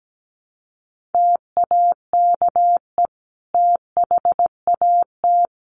Morse code, heard as a keyed tone, about 17 wpm, 700 Hz. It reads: TAKE THAT
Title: Sentences from Top 1300 Words 17wpm